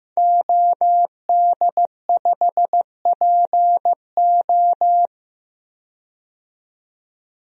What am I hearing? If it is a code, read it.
Morse code, heard as a keyed tone, about 15 wpm, 700 Hz. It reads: OD5PO
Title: DXCC-100 Call Signs - 5 Characters - 15wpm